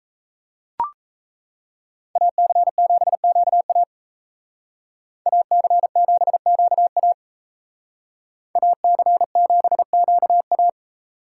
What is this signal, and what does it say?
Morse code, heard as a keyed tone, about 36 wpm, 700 Hz. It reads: AC7QA AC7QA AC7QA